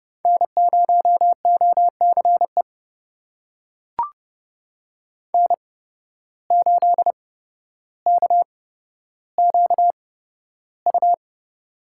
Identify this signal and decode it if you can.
Morse code, heard as a keyed tone, about 30 wpm, spaced as for 13 wpm, 700 Hz. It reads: N0OCE N8KQU
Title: USA Call Signs - 8x Character Spacing - 30wpm